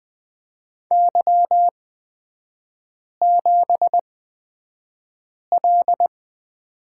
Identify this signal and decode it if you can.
Morse code, heard as a keyed tone, about 20 wpm, spaced as for 5 wpm, 700 Hz. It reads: Y7L
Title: 243 Lesson 39e - Letter z - Three Characters 20wpm